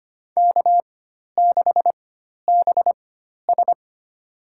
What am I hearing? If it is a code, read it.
Morse code, heard as a keyed tone, about 25 wpm, spaced as for 12 wpm, 700 Hz. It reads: K6BS
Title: USA Call Signs - 4 Characters - 4x Character Spacing - 25wpm